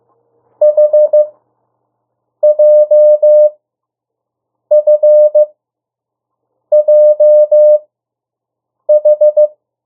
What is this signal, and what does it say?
Morse code, heard as a keyed tone, about 15 wpm, spaced as for 6 wpm, 600 Hz. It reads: HJFJH
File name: morse mp3 / 02 urok 1